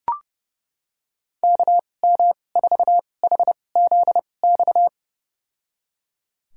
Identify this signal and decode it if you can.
Morse code, heard as a keyed tone, about 30 wpm, spaced as for 22 wpm, 700 Hz. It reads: KM4HZX
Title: USA Call Signs - 6 Characters - 2x Character Spacing - 30wpm